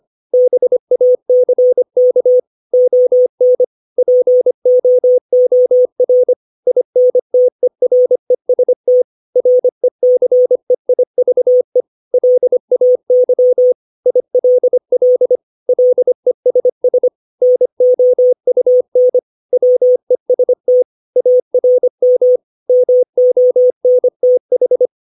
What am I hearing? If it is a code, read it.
Morse code, heard as a keyed tone, about 25 wpm, 500 Hz. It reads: BACK ON POOR INTEREST RECEIVE LAY ILL LESS NOUN WEST ARM MONTH
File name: 632 words / 632 Words 25WPM